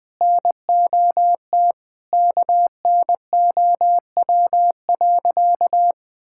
Morse code, heard as a keyed tone, about 21 wpm, 700 Hz. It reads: NOT KNOW.
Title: Sentences with 4-Word Phrases - Spoken Beginning - 20wpm